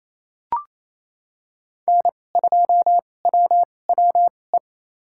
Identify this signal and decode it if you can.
Morse code, heard as a keyed tone, about 28 wpm, spaced as for 20 wpm, 700 Hz. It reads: N2WWE
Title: USA Call Signs - 5 Characters - 2x Character Spacing - 28wpm